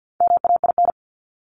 Morse code, heard as a keyed tone, about 48 wpm, 700 Hz. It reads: MAIN
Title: Top 300 Words 50wpm